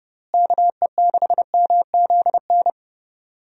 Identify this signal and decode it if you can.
Morse code, heard as a keyed tone, about 30 wpm, 700 Hz. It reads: KE6MZN
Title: USA Call Signs - 6 Characters - Repeated 3x - 30wpm